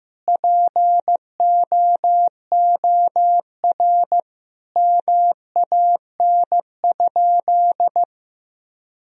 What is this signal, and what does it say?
Morse code, heard as a keyed tone, about 15 wpm, 700 Hz. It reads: POOR MAN?
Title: Sentences from Top 1000 Words - Review Part 2 - 15wpm